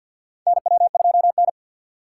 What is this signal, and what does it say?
Morse code, heard as a keyed tone, about 51 wpm, 700 Hz. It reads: NW2N
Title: USA Call Signs - 4 Characters - 2x Character Spacing - 50wpm